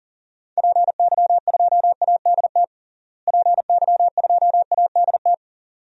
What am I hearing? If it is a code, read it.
Morse code, heard as a keyed tone, about 40 wpm, 700 Hz. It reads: PY2ADT PY2ADT
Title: DXCC-100 Call Signs - 6 Characters - Repeated 2x - 40wpm